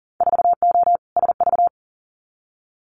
Morse code, heard as a keyed tone, about 40 wpm, 700 Hz. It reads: 4O SV
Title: Winter Field Day Exchanges 40wpm